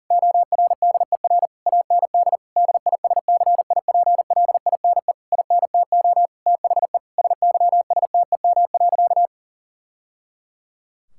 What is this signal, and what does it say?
Morse code, heard as a keyed tone, about 37 wpm, 700 Hz. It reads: ORDER AND DISCIPLINE INTO THE SYSTEM.
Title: Sentences from Top 1000 Words - Review Part 6 - 40wpm